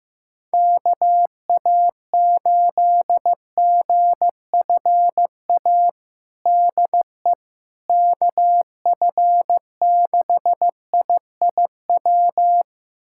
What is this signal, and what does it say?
Morse code, heard as a keyed tone, about 15 wpm, 700 Hz. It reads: KA8GFA DE KF6IIW
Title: USA Callsign DE Callsign 15wpm